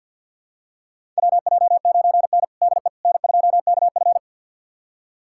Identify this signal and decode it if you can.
Morse code, heard as a keyed tone, about 52 wpm, 700 Hz. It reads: WJ9N DE N2KF
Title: USA Callsign DE Callsign - 4 Characters - 50wpm